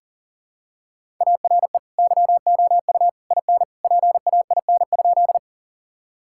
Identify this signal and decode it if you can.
Morse code, heard as a keyed tone, about 40 wpm, 700 Hz. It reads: ARE YOU IN PAIN?
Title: Sentences from Top 1600 Words 40wpm